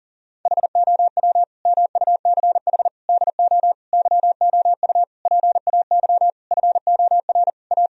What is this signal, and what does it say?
Morse code, heard as a keyed tone, about 40 wpm, 700 Hz. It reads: HOW MUCH DO YOU PAY FOR A
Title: Sentences from Top 1000 Words - Review Part 3 - 40wpm